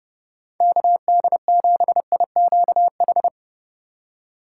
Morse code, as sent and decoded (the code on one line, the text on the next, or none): -.- -.. --... .. --.- ....
KD7IQH